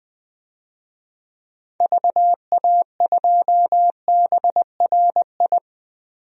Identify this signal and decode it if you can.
Morse code, heard as a keyed tone, about 20 wpm, 700 Hz. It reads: VA2BRI
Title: DXCC-100 Call Signs - 6 Characters - 20wpm